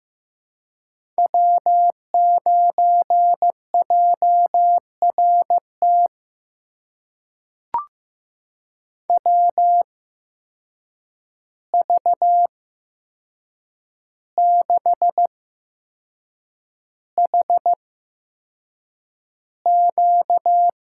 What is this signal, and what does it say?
Morse code, heard as a keyed tone, about 15 wpm, spaced as for 6 wpm, 700 Hz. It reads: W9JRT WV6HQ
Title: USA Call Signs - 5 Characters - 8x Character Spacing - 15wpm